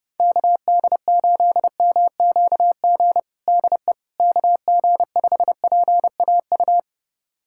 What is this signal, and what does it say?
Morse code, heard as a keyed tone, about 32 wpm, 700 Hz. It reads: KD8MQG DE KG5PAU
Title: USA Callsign DE Callsign 30wpm